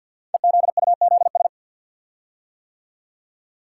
Morse code, heard as a keyed tone, about 46 wpm, 700 Hz. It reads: EZUZS
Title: Code Groups - 5 Letters - 50wpm